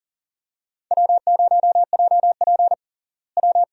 Morse code, heard as a keyed tone, about 37 wpm, 700 Hz. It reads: W0JP W
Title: USA Call Signs - 4 Characters - Repeated 2x - 40wpm